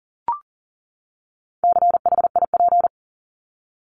Morse code, heard as a keyed tone, about 40 wpm, 700 Hz. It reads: CHIP
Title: Top 1751-2000 Words 40wpm